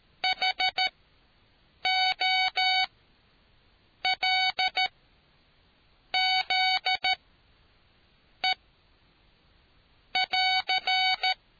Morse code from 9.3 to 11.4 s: <AR>